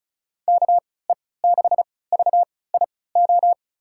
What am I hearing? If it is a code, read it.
Morse code, heard as a keyed tone, about 36 wpm, spaced as for 20 wpm, 700 Hz. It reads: KE6VIO
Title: USA Call Signs - 6 Characters - 3x Character Spacing - 35wpm